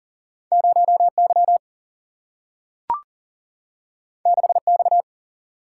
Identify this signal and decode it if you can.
Morse code, heard as a keyed tone, about 40 wpm, 700 Hz. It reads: 0 Y 6 X
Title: Single-Letter Single-Number 40wpm